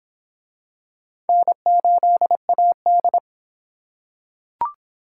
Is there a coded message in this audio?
Yes: Morse code from 1.3 to 3.2 s, heard as a keyed tone, about 25 wpm, 700 Hz. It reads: N8AD